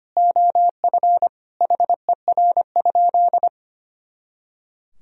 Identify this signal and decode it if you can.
Morse code, heard as a keyed tone, about 25 wpm, 700 Hz. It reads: OF HER?
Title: Sentences from Top 200 Words and Abbreviations 25wpm